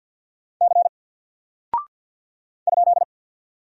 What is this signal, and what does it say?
Morse code, heard as a keyed tone, about 48 wpm, 700 Hz. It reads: C ?
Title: Single Character (Letter:Number:Punctuation:Prosign) 50wpm